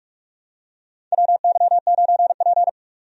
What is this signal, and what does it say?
Morse code, heard as a keyed tone, about 46 wpm, 700 Hz. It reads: WY9P
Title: USA Call Signs - 4 Characters - 45wpm